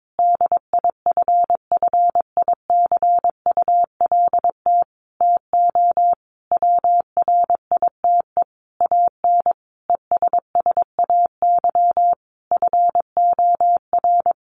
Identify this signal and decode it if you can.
Morse code, heard as a keyed tone, about 22 wpm, 700 Hz. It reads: DIFFICULT TO WRITE AN ESSAY FOR
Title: Sentences from Top 1500 Words 22wpm